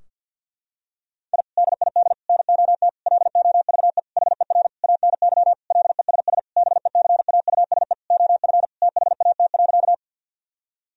Morse code, heard as a keyed tone, about 51 wpm, 700 Hz. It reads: I DID NOT LOVE HER ANY LESS BECAUSE OF THAT.
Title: Sentences from Top 1000 Words - Review Part 2 - 50wpm